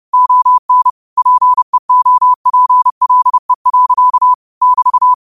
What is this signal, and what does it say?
Morse code, heard as a keyed tone, about 30 wpm, 1000 Hz. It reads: ON PEOPLE. <BT>
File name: qotdfiles / 30wpm